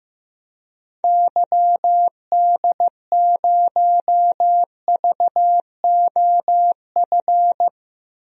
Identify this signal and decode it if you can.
Morse code, heard as a keyed tone, about 15 wpm, 700 Hz. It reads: YD0VOF
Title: DXCC-100 Call Signs - 6 Characters - 15wpm